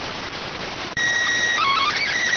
SSTV (slow-scan television), PD 290.